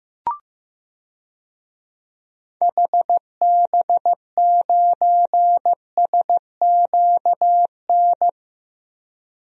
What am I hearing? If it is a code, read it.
Morse code, heard as a keyed tone, about 15 wpm, 700 Hz. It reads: HB9SQN
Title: DXCC-100 Call Signs - 6 Characters - Repeated 2x - 15wpm